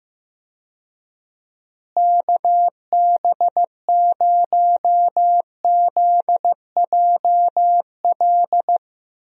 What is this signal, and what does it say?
Morse code, heard as a keyed tone, about 15 wpm, 700 Hz. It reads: KB0ZJL
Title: USA Call Signs - 6 Characters - Repeated 2x - 15wpm